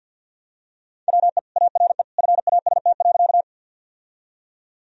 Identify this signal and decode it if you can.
Morse code, heard as a keyed tone, about 50 wpm, 700 Hz. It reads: WE ARE FAST.
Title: Sentences from Top 1000 Words - Review Part 2 - 50wpm